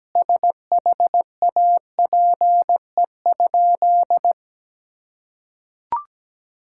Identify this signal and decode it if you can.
Morse code, heard as a keyed tone, about 17 wpm, 700 Hz. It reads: SHAPE?
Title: Sentences from Top 1000 Words - Review Part 6 - 17wpm